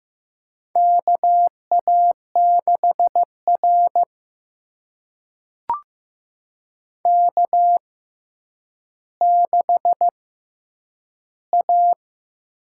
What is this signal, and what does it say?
Morse code, heard as a keyed tone, about 15 wpm, spaced as for 9 wpm, 700 Hz. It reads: KA6R K6A
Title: USA Call Signs - 4 Characters - 6x Character Spacing - 15wpm